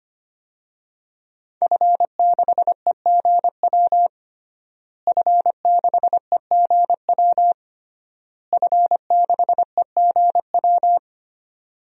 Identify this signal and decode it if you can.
Morse code, heard as a keyed tone, about 25 wpm, 700 Hz. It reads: F6EGW F6EGW F6EGW
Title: DXCC-100 Call Signs - 5 Characters - Repeated 3x - 25wpm